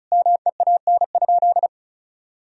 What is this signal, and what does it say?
Morse code, heard as a keyed tone, about 36 wpm, 700 Hz. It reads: MEAN?
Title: Sentences from Top 200 Words and Abbreviations 35wpm